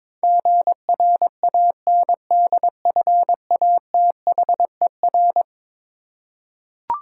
Morse code, heard as a keyed tone, about 22 wpm, 700 Hz. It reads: GRANDFATHER
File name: practice / 22wpm / Eleven Letter Words - 22wpm